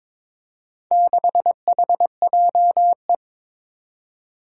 Morse code, heard as a keyed tone, about 23 wpm, 700 Hz. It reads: 6HJE
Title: Four Challenging Letter-Number Combinations 22wpm